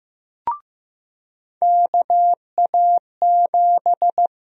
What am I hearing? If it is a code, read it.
Morse code, heard as a keyed tone, about 15 wpm, 700 Hz. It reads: KA7
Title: USA Call Sign Prefixes - 15wpm